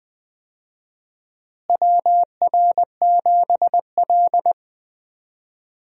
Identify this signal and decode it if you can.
Morse code, heard as a keyed tone, about 20 wpm, 700 Hz. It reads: WR7L
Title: USA Call Signs - 4 Characters - Repeated 3x - 20wpm